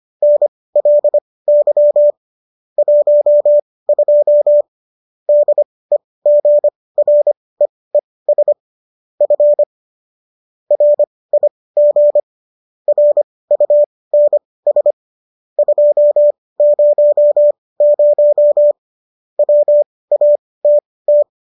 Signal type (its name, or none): Morse code